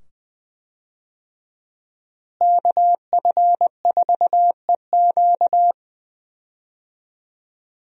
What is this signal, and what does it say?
Morse code, heard as a keyed tone, about 20 wpm, 700 Hz. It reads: KF4EQ
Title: USA Call Signs - 5 Characters - Repeated 2x - 20wpm